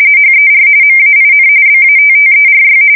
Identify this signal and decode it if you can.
RTTY, about 45.45 baud. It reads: HER MAIN BATTERY
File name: fsk170 / battleshipnj